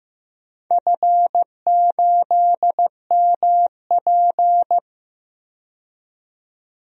Morse code, heard as a keyed tone, about 15 wpm, 700 Hz. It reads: F8MP
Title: DXCC-100 Call Signs - 4 Characters - 15wpm